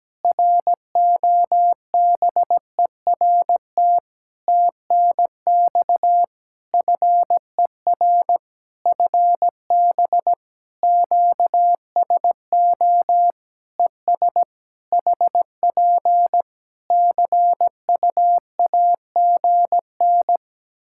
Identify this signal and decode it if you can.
Morse code, heard as a keyed tone, about 17 wpm, 700 Hz. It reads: ROBERT TNX FER FB QSO ES HP CUAGN
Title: K1USN SST QSO Practice 17wpm